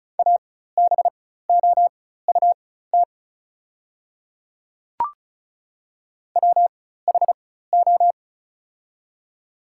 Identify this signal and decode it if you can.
Morse code, heard as a keyed tone, about 36 wpm, spaced as for 16 wpm, 690 Hz. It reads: ABOUT WHO
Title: 4x Character Spacing - Top 100 Words - 35wpm